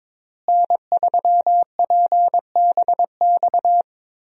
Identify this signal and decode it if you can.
Morse code, heard as a keyed tone, about 22 wpm, 700 Hz. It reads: N3PBX
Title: USA Call Signs - 5 Characters - Repeated 2x - 22wpm